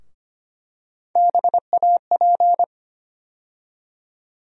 Morse code, heard as a keyed tone, about 25 wpm, 700 Hz. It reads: BAP
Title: USA Call Sign Postfixes - 3 Letters 25wpm